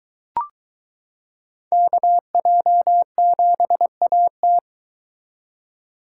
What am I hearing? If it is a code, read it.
Morse code, heard as a keyed tone, about 22 wpm, 700 Hz. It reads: KJ7AT